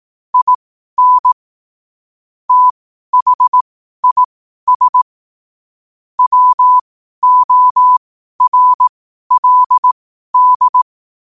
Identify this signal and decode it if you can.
Morse code, heard as a keyed tone, about 18 wpm, spaced as for 12 wpm, 1000 Hz. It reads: IN THIS WORLD
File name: qotdfiles / 12wpm